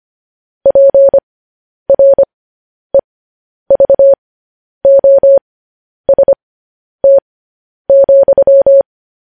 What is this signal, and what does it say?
Morse code, heard as a keyed tone, about 25 wpm, spaced as for 10 wpm, 550 Hz. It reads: PREVOST,
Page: News Headlines in Morse Code at 10 WPM